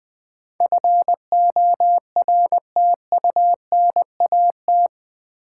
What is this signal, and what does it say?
Morse code, heard as a keyed tone, about 20 wpm, 700 Hz. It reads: FORTUNAT
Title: Build A Word - 4 Letters At A Time - 11-Letter Words - 20wpm